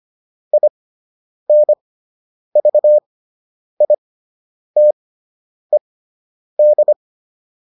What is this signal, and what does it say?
Morse code, heard as a keyed tone, about 25 wpm, spaced as for 9 wpm, 600 Hz. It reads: INVITED